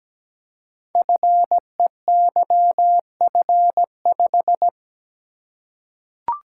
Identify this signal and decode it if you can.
Morse code, heard as a keyed tone, about 17 wpm, 700 Hz. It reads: FEYF5
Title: Code Groups - 5 Letter:Numbers - 17wpm